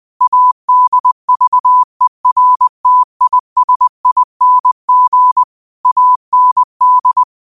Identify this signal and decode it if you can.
Morse code, heard as a keyed tone, about 20 wpm, 1000 Hz. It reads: ADVERTISING AND
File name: 20wpm